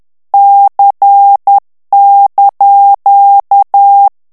Morse code, heard as a keyed tone, about 10 wpm, 800 Hz. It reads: C)
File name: ITALIEN texte clair 2 vitesse 10 mots minute